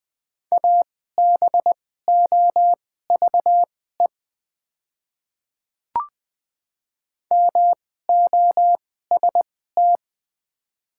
Morse code, heard as a keyed tone, about 20 wpm, spaced as for 14 wpm, 700 Hz. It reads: ABOVE MOST
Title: 2x Character Spacing - Top 500 Words - 20wpm